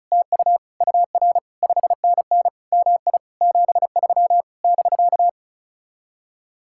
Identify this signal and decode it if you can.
Morse code, heard as a keyed tone, about 45 wpm, 700 Hz. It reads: TU UR 5NN MI 73 <BK>